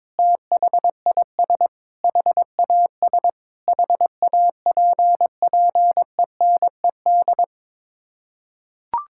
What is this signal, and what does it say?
Morse code, heard as a keyed tone, about 22 wpm, 700 Hz. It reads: THIS HAS HAPPENED
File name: practice / 22wpm / Sets of 8 Words - 22wpm